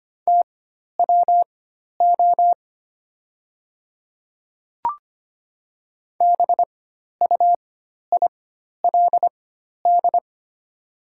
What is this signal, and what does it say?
Morse code, heard as a keyed tone, about 25 wpm, spaced as for 12 wpm, 700 Hz. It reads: TWO BUILD
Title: 4x Character Spacing - Top 200 Words - 25wpm